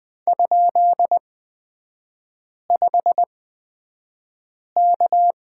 Morse code, heard as a keyed tone, about 20 wpm, spaced as for 5 wpm, 700 Hz. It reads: ?5K
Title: 243 Lesson 39e - Letter z - Three Characters 20wpm